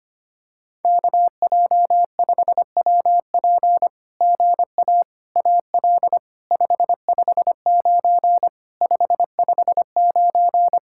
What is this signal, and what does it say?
Morse code, heard as a keyed tone, about 24 wpm, 700 Hz. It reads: KJ5WP GA AL 559 559
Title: SOTA QSO Practice 25wpm